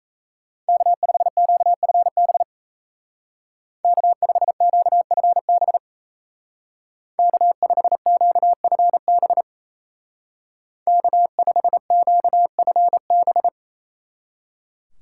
Morse code, heard as a keyed tone, about 35 wpm, 700 Hz. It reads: K5QFB K5QFB K5QFB K5QFB
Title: Speed Racing - USA Call Signs 28wpm